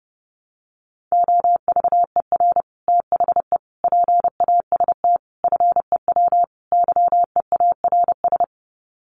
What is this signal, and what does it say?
Morse code, heard as a keyed tone, about 30 wpm, 700 Hz. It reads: OVER THE PAST FEW YEARS
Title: Sets of 5 Words Encore 30wpm